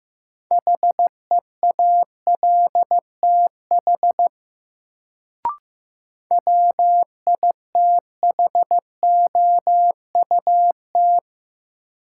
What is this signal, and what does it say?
Morse code, heard as a keyed tone, about 15 wpm, 700 Hz. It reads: HEALTH WITHOUT
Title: Sentences from Top 600 Words - New Word Drill - 15wpm